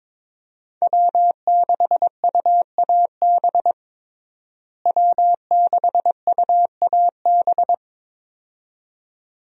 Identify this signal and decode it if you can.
Morse code, heard as a keyed tone, about 21 wpm, 700 Hz. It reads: W6UAB W6UAB
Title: USA Call Signs - 5 Characters - Repeated 2x - 22wpm